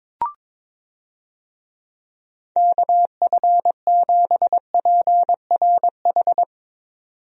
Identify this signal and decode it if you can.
Morse code, heard as a keyed tone, about 22 wpm, 700 Hz. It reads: KF7PRH
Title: USA Call Signs - 6 Characters - 22wpm